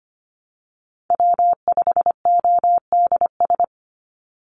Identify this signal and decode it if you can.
Morse code, heard as a keyed tone, about 25 wpm, 700 Hz. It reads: W5ODS
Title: NA SOTA Activator Callsigns 25wpm